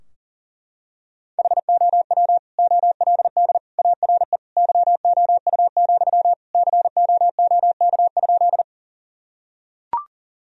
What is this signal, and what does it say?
Morse code, heard as a keyed tone, about 37 wpm, 700 Hz. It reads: HOW OLD ARE YOU, COOK?
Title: Sentences from Top 1000 Words - Review Part 4 - 40wpm